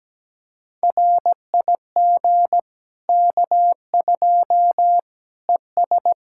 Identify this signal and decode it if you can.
Morse code, heard as a keyed tone, about 17 wpm, 700 Hz. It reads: RIG K2 ES
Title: Practice QSO - Exchange 2 - Info - 17wpm